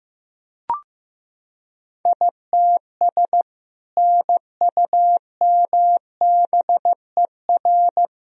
Morse code, heard as a keyed tone, about 15 wpm, 700 Hz. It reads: ITS NUMBER
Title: Sentences from Top 1000 Words - Review Part 1 - 15wpm